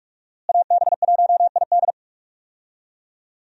Morse code, heard as a keyed tone, about 44 wpm, 700 Hz. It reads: AB1ID
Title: USA Call Signs - 5 Characters - Repeated 3x - 45wpm